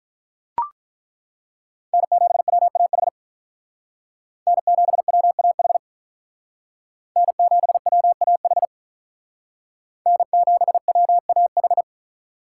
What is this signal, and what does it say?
Morse code, heard as a keyed tone, about 41 wpm, 700 Hz. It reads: N7WAH N7WAH N7WAH N7WAH
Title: Speed Racing - USA 1x3 Call Signs 35wpm